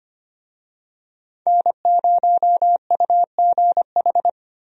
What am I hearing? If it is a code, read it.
Morse code, heard as a keyed tone, about 25 wpm, 700 Hz. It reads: N0UGH
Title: USA Call Signs - 5 Characters - Repeated 3x - 25wpm